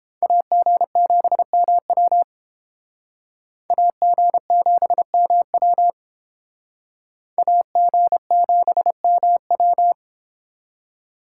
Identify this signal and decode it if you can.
Morse code, heard as a keyed tone, about 31 wpm, 700 Hz. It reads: AG7MW AG7MW AG7MW